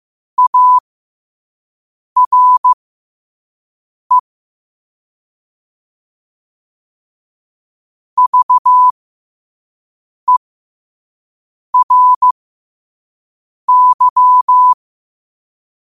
Morse code, heard as a keyed tone, about 15 wpm, spaced as for 5 wpm, 1000 Hz. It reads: ARE VERY
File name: qotdfiles / 5wpm